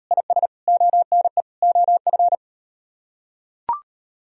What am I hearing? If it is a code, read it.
Morse code, heard as a keyed tone, about 39 wpm, 700 Hz. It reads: IS ONE OF